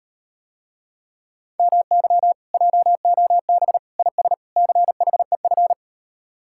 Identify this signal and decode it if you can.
Morse code, heard as a keyed tone, about 38 wpm, 700 Hz. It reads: MY JOB IS CHEF